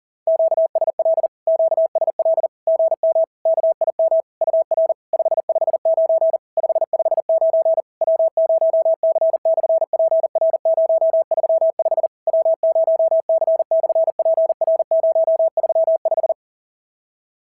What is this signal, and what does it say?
Morse code, heard as a keyed tone, about 39 wpm, 650 Hz. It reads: QSL QSL GM KIM UR 559 559 W0C/PR035 W0C/PR035
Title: SOTA QSO Practice 40wpm